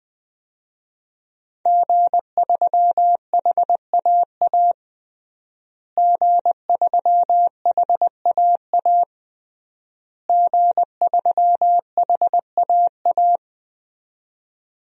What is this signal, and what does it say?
Morse code, heard as a keyed tone, about 21 wpm, 700 Hz. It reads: G3HAA G3HAA G3HAA
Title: DXCC-100 Call Signs - 5 Characters - Repeated 3x - 20wpm